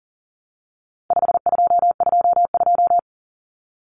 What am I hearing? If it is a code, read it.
Morse code, heard as a keyed tone, about 40 wpm, 700 Hz. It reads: 5222
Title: Numbers - 4-Digit 40wpm